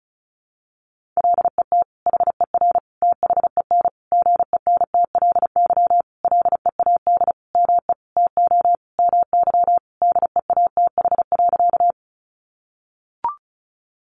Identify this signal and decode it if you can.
Morse code, heard as a keyed tone, about 34 wpm, 700 Hz. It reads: LET HER THEN GENTLY LEAD ME TO MY DEATH.
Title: Sentences from Top 1500 Words 35wpm